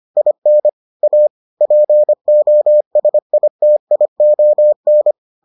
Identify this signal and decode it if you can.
Morse code, heard as a keyed tone, about 25 wpm, 600 Hz. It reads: IN A POSITION